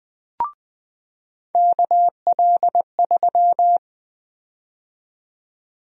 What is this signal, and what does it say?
Morse code, heard as a keyed tone, about 20 wpm, 700 Hz. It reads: KL3
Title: USA Call Sign Prefixes - 20wpm